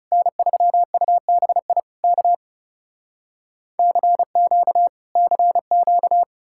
Morse code, heard as a keyed tone, about 37 wpm, 700 Hz. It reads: N3UBI K CQ CQ
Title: Speed-Racing Common QSO Phrases - 30wpm